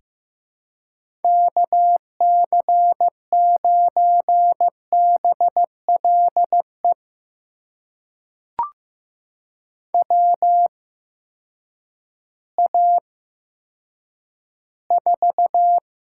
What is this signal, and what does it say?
Morse code, heard as a keyed tone, about 15 wpm, spaced as for 9 wpm, 700 Hz. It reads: KC9BLE WA4
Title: USA Call Signs - 6 Characters - 8x Character Spacing - 15wpm